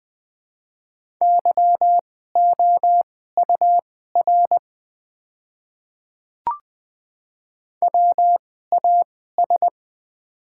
Morse code, heard as a keyed tone, about 20 wpm, spaced as for 14 wpm, 700 Hz. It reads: YOUR WAS
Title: 2x Character Spacing - Top 100 Words - 20wpm